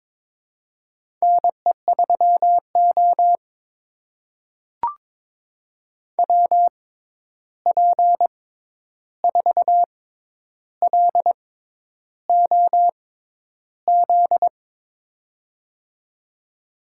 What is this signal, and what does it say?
Morse code, heard as a keyed tone, about 22 wpm, spaced as for 10 wpm, 700 Hz. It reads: NE3O WP4LOZ
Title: USA Call Signs - 6x Character Spacing - 22wpm